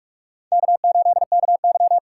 Morse code, heard as a keyed tone, about 46 wpm, 700 Hz. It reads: K8KY